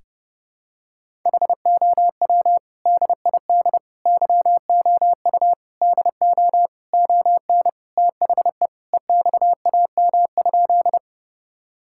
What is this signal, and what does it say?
Morse code, heard as a keyed tone, about 32 wpm, 700 Hz. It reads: HOW DID YOU DO ON THE EXAM?
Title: Sentences from Top 1000 Words - Review Part 5 - 30wpm